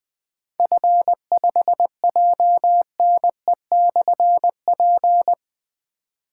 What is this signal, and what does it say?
Morse code, heard as a keyed tone, about 20 wpm, 700 Hz. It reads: F5JNE/P
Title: Difficult Call Signs 20wpm